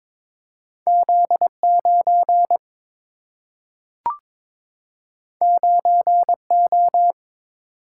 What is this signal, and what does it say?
Morse code, heard as a keyed tone, about 22 wpm, 700 Hz. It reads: Z 9 9 O
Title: Single-Letter Single-Number 22wpm